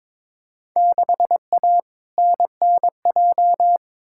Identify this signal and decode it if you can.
Morse code, heard as a keyed tone, about 22 wpm, 700 Hz. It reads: 6A NNJ
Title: Field Day Exchanges Encore - 22wpm